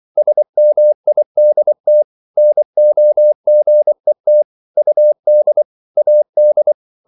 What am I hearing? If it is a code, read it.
Morse code, heard as a keyed tone, about 24 wpm, 600 Hz. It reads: SMIDT NOGET UD AD
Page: Ferie 24 wpm